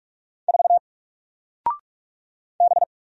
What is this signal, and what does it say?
Morse code, heard as a keyed tone, about 44 wpm, 700 Hz. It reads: 4 B
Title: Single Letter-Number Alt 45wpm